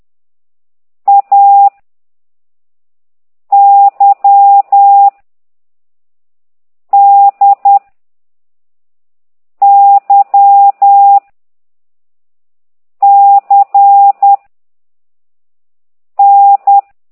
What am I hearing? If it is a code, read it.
Morse code, heard as a keyed tone, about 10 wpm, 800 Hz. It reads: A Y D Y C N
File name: exerc7